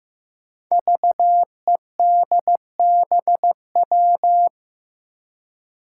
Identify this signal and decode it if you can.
Morse code, heard as a keyed tone, about 15 wpm, 700 Hz. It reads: VEDBW
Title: Five Challenging Letter-Number Combinations 15wpm